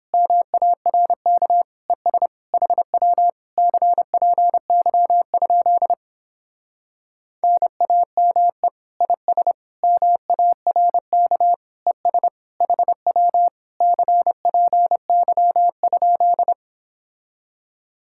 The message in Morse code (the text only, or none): MARK ES HW CPY? NAME IS MARK ES HW CPY?